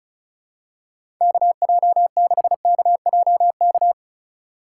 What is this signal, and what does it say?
Morse code, heard as a keyed tone, about 36 wpm, 700 Hz. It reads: KJ6KJK
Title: USA Call Signs - 6 Characters - 3x Character Spacing - 35wpm